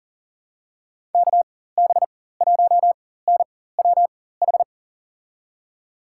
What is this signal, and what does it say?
Morse code, heard as a keyed tone, about 40 wpm, spaced as for 19 wpm, 700 Hz. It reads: KB1NWH
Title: USA Call Signs - 6 Characters - 4x Character Spacing - 40wpm